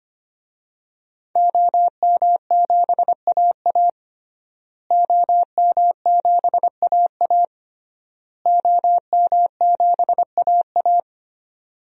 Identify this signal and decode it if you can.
Morse code, heard as a keyed tone, about 25 wpm, 700 Hz. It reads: OM7AA OM7AA OM7AA
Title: DXCC-100 Call Signs - 5 Characters - Repeated 3x - 25wpm